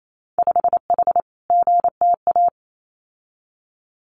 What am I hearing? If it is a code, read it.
Morse code, heard as a keyed tone, about 28 wpm, 700 Hz. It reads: 5H GTA
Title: Winter Field Day Exchanges Encore 28wpm